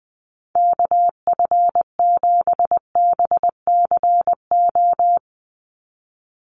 Morse code, heard as a keyed tone, about 20 wpm, 700 Hz. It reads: KF7BCO
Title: USA Call Signs 20wpm